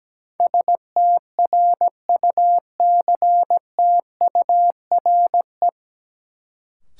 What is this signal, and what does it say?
Morse code, heard as a keyed tone, about 17 wpm, 700 Hz. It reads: STRUCTURE
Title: Sentences from Top 1000 Words - Review Part 6 - New Word Drill - 17wpm